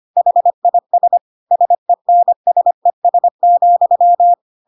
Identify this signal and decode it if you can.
Morse code, heard as a keyed tone, about 25 wpm, 700 Hz. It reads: HIS SENSES,
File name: Pinocchio / Pinocchio ch1 25 25 ik1hge